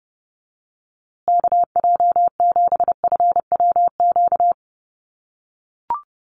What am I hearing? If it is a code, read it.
Morse code, heard as a keyed tone, about 30 wpm, 700 Hz. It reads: KJ7FWQ
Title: USA Call Signs Repeated 2x 30wpm